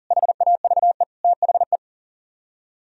Morse code, heard as a keyed tone, about 52 wpm, 700 Hz. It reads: HAVE THE